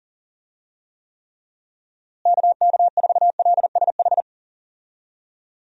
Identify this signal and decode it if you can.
Morse code, heard as a keyed tone, about 37 wpm, 700 Hz. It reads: KK4LSH
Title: USA Call Signs - 6 Characters - Repeated 3x - 40wpm